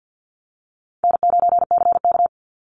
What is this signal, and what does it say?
Morse code, heard as a keyed tone, about 50 wpm, 700 Hz. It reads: N9CK
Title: Easy Call Signs - 50wpm